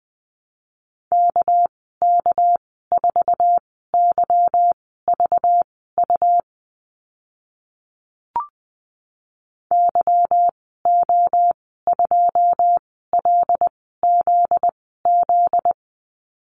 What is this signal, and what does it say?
Morse code, heard as a keyed tone, about 20 wpm, spaced as for 14 wpm, 700 Hz. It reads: KK4YVU YO2LZZ
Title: 245 Lesson 39g - Letter z - Call Signs 20wpm